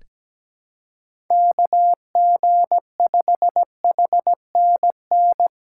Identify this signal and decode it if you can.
Morse code, heard as a keyed tone, about 17 wpm, 700 Hz. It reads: KG5HNN
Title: USA Call Signs - 6 Characters - 8x Character Spacing - 17wpm